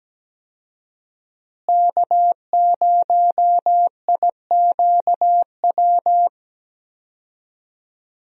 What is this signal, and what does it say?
Morse code, heard as a keyed tone, about 17 wpm, 700 Hz. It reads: K0IQW
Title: USA Call Signs - 5 Characters - Repeated 2x - 17wpm